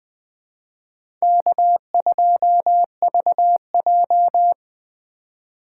Morse code, heard as a keyed tone, about 20 wpm, 700 Hz. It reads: K2VJ
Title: Speed Racing - USA 1x2 Call Signs 17wpm